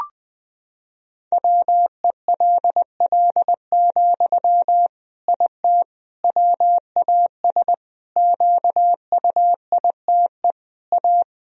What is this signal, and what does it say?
Morse code, heard as a keyed tone, about 21 wpm, 700 Hz. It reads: WELL, IT WAS QUITE A
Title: Sentences from Top 1000 Words - Review Part 3 - 20wpm